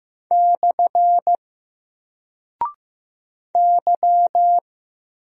Morse code, heard as a keyed tone, about 15 wpm, 700 Hz. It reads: / Y
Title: Single Character (Letter:Number:Punctuation:Prosign) Alt 15wpm